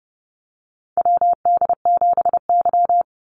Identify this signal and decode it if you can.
Morse code, heard as a keyed tone, about 31 wpm, 700 Hz. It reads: WD7Y
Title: NA SOTA Chaser Callsigns 30wpm